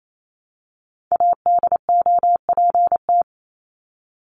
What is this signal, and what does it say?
Morse code, heard as a keyed tone, about 28 wpm, 700 Hz. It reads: ADOPT
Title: ED Suffix - Top 100 Words with Base Words 28wpm